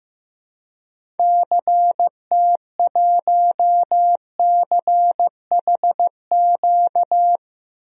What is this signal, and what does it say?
Morse code, heard as a keyed tone, about 16 wpm, 700 Hz. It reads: CT1CHQ
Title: DXCC-100 Call Signs - 6 Characters - 15wpm